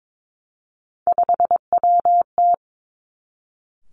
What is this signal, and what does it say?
Morse code, heard as a keyed tone, about 22 wpm, 700 Hz. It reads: 5WT
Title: Three Challenging Letter-Number Combinations 22wpm